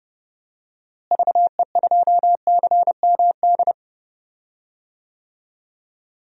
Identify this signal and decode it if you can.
Morse code, heard as a keyed tone, about 31 wpm, 700 Hz. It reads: VE2CMD
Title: DXCC-100 Call Signs - 6 Characters - Repeated 3x - 30wpm